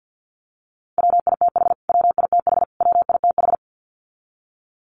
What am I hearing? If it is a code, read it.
Morse code, heard as a keyed tone, about 48 wpm, 700 Hz. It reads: WITH WITH WITH
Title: Top 300 Words Repeated 50wpm